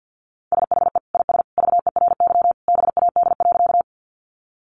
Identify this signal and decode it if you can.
Morse code, heard as a keyed tone, about 50 wpm, 700 Hz. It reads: SHE IS VERY BAD.
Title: Sentences from Top 700 Words 50wpm